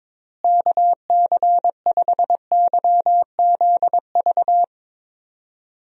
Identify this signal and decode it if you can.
Morse code, heard as a keyed tone, about 22 wpm, 700 Hz. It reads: KC5YZV
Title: USA Call Signs - 6 Characters - Repeated 3x - 22wpm